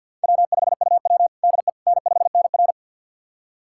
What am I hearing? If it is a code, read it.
Morse code, heard as a keyed tone, about 50 wpm, 700 Hz. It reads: W5UW DE N5NR
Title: USA Callsign DE Callsign - 4 Characters - 50wpm